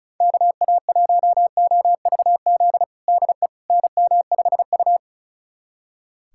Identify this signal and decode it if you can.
Morse code, heard as a keyed tone, about 36 wpm, 700 Hz. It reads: KA1OVZ DE NM5U
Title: USA Callsign DE Callsign 35wpm